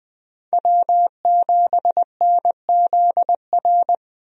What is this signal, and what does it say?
Morse code, heard as a keyed tone, about 20 wpm, 700 Hz. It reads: W7NZR
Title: USA Call Signs - 5 Characters - 20wpm